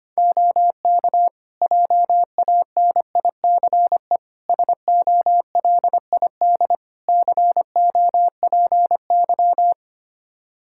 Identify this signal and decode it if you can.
Morse code, heard as a keyed tone, about 25 wpm, 700 Hz. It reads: OK JANICE SOLID COPY
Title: Practice QSO - Exchange 3 - More Info - 25wpm